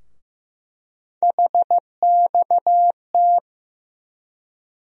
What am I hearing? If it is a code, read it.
Morse code, heard as a keyed tone, about 15 wpm, 700 Hz. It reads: HXT
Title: USA Call Sign Postfixes - 3 Letters 15wpm